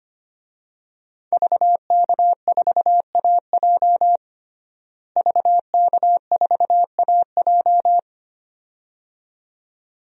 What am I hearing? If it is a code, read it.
Morse code, heard as a keyed tone, about 25 wpm, 700 Hz. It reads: VK4AJ VK4AJ
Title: DXCC-100 Call Signs - 5 Characters - Repeated 2x - 25wpm